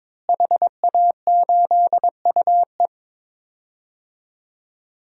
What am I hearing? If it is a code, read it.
Morse code, heard as a keyed tone, about 22 wpm, 700 Hz. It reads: HA8UE
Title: DXCC-100 Call Signs - 5 Characters - Repeated 2x - 22wpm